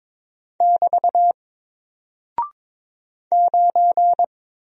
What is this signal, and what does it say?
Morse code, heard as a keyed tone, about 22 wpm, 700 Hz. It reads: <BT> 9
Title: Single Character (Letter:Number:Punctuation:Prosign) Alt 22wpm